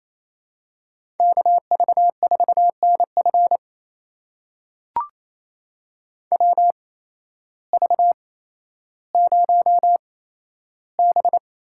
Morse code, heard as a keyed tone, about 28 wpm, spaced as for 13 wpm, 700 Hz. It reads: KV4NF WV0B
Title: USA Call Signs - 8x Character Spacing - 28wpm